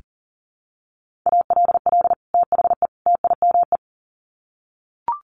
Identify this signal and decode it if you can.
Morse code, heard as a keyed tone, about 40 wpm, 700 Hz. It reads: ALL THE TIME
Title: Sets of 3 Words 40wpm